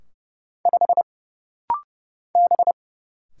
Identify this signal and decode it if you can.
Morse code, heard as a keyed tone, about 29 wpm, 700 Hz. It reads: 5 B
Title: Single Letter-Number Alt - Rapid-Fire 30wpm